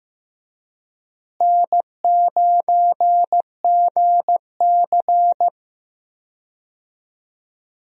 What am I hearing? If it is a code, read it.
Morse code, heard as a keyed tone, about 15 wpm, 700 Hz. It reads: N9GC
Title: USA Call Signs - 4 Characters - Repeated 3x - 15wpm